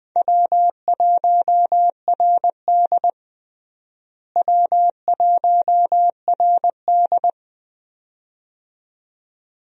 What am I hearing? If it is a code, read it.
Morse code, heard as a keyed tone, about 20 wpm, 700 Hz. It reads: W1RD W1RD
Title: USA Call Signs - 4 Characters - Repeated 2x - 20wpm